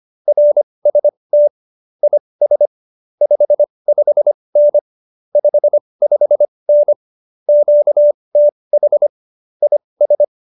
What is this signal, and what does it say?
Morse code, heard as a keyed tone, about 25 wpm, spaced as for 20 wpm, 590 Hz. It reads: RST IS 55N 55N QTH IS